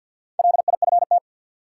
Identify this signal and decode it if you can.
Morse code, heard as a keyed tone, about 50 wpm, 700 Hz. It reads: LIFT
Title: Sentences from Top 1000 Words - New Word Drill - 50wpm